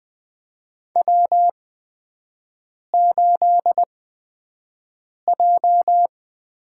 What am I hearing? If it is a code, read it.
Morse code, heard as a keyed tone, about 20 wpm, spaced as for 5 wpm, 700 Hz. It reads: W8J
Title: USA Call Signs - 5 Characters - 8x Character Spacing - 20wpm